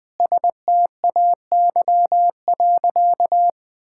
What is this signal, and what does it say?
Morse code, heard as a keyed tone, about 21 wpm, 700 Hz. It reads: STAY.
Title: Sentences with 5-Word Phrases - 20wpm